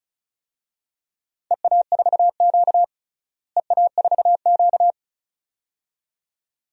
Morse code, heard as a keyed tone, about 34 wpm, 700 Hz. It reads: EA4Q EA4Q
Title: DXCC-100 Call Signs - 4 Characters - Repeated 2x - 35wpm